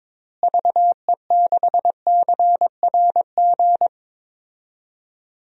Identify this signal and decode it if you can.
Morse code, heard as a keyed tone, about 22 wpm, 700 Hz. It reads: VE6CRG
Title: DXCC-100 Call Signs - 6 Characters - Repeated 3x - 22wpm